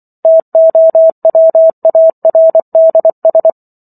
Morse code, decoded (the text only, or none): TOWARDS